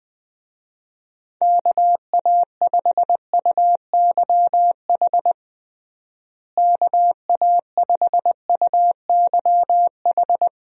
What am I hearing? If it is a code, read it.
Morse code, heard as a keyed tone, about 20 wpm, 700 Hz. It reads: KA5UYH KA5UYH
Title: USA Call Signs - 6 Characters - Repeated 3x - 20wpm